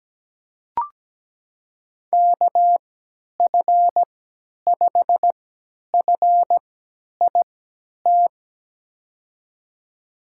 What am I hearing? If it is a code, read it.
Morse code, heard as a keyed tone, about 17 wpm, spaced as for 10 wpm, 700 Hz. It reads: KF5FIT
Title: USA Call Signs - 6 Characters - 3x Character Spacing - 17wpm